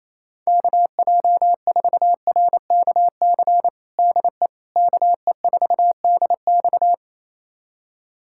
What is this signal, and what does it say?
Morse code, heard as a keyed tone, about 28 wpm, 700 Hz. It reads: KJ4RKC DE KE4DX
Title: USA Callsign DE Callsign 28wpm